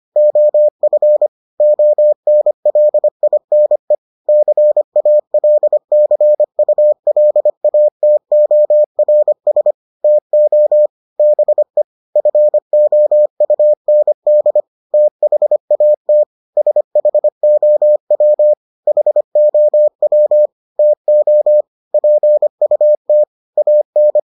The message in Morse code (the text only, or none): OF ONLINE CALCULATORS TO BE FOUND THAT SHOW HOW TO PUT AN